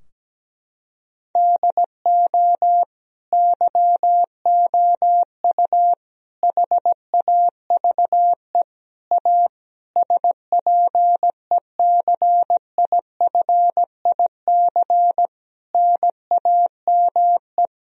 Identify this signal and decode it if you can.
Morse code, heard as a keyed tone, about 17 wpm, 700 Hz. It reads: DO YOU HAVE A SPECIFIC NAME
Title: Sentences from Top 1000 Words - Review Part 3 - 17wpm